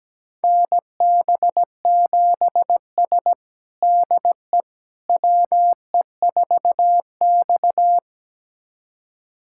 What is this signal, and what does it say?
Morse code, heard as a keyed tone, about 17 wpm, 700 Hz. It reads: NB7S DE WE4X
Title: USA Callsign DE Callsign - 4 Characters - 17wpm